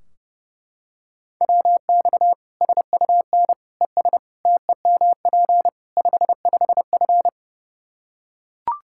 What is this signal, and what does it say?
Morse code, heard as a keyed tone, about 32 wpm, 700 Hz. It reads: WX SUN ES TEMP 55F
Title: K1USN SST QSO Practice 30wpm